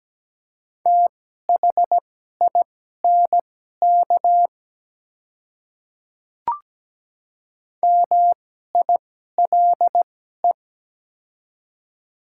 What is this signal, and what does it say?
Morse code, heard as a keyed tone, about 17 wpm, spaced as for 12 wpm, 700 Hz. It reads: THINK MILE
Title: 2x Character Spacing - Top 300 Words - 17wpm